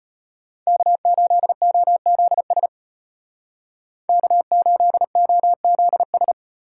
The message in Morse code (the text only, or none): K8OZS K8OZS